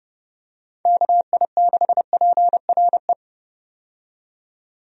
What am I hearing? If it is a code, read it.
Morse code, heard as a keyed tone, about 30 wpm, 700 Hz. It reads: KI6PRE
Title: USA Call Signs - 6 Characters - 30wpm